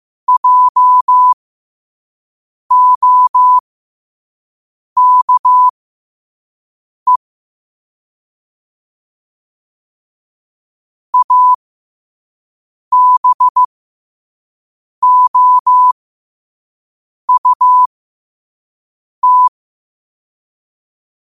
Morse code, heard as a keyed tone, about 15 wpm, spaced as for 5 wpm, 1000 Hz. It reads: JOKE ABOUT
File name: qotdfiles / 5wpm